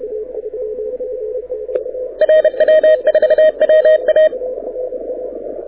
Morse code, heard as a keyed tone, about 31 wpm, 620 Hz, with radio noise. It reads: RW4WA
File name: audio zawody / IARU 2015 CW SN7Q